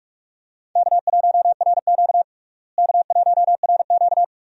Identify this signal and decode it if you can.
Morse code, heard as a keyed tone, about 46 wpm, 700 Hz. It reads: K1RQ K1RQ
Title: USA Call Signs - 4 Characters - Repeated 2x - 45wpm